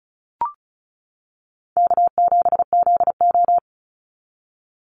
Morse code, heard as a keyed tone, about 34 wpm, 700 Hz. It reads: K7ZO
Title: NA SOTA Chaser Callsigns 35wpm